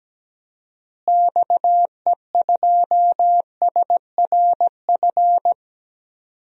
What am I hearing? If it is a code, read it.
Morse code, heard as a keyed tone, about 17 wpm, 700 Hz. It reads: XE2SRF
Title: DXCC-100 Call Signs - 6 Characters - Repeated 3x - 17wpm